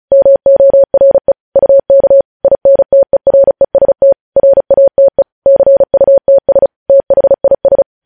Morse code, heard as a keyed tone, about 36 wpm, 550 Hz. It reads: MORE UK INTEREST RATE CUTS THIS
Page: news headlines in morse code at 35 wpm